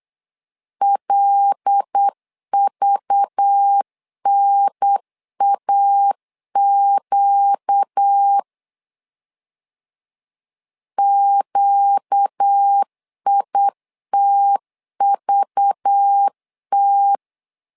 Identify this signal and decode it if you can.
Morse code, heard as a keyed tone, about 8 wpm, 800 Hz. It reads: LVNAQ QITVT
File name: LEZIONI / 5 LBVC / 5 LBV 8wpm 2837